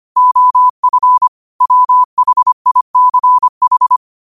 Morse code, heard as a keyed tone, about 25 wpm, 1000 Hz. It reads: OF WHICH
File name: qotdfiles / 25wpm